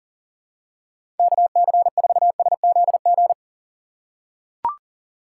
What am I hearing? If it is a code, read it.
Morse code, heard as a keyed tone, about 40 wpm, 700 Hz. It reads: KC4SZG
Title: USA Call Signs - 6 Characters - 4x Character Spacing - 40wpm